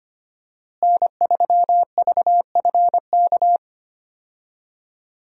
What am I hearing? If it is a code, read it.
Morse code, heard as a keyed tone, about 25 wpm, 700 Hz. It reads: N3VFK
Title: USA Call Signs - 5 Characters - Repeated 2x - 25wpm